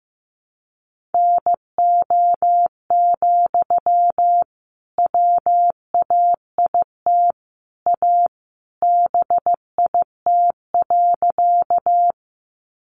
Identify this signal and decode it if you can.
Morse code, heard as a keyed tone, about 15 wpm, 700 Hz. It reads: NO, WAIT A BIT.
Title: Sentences from Top 900 Words 15wpm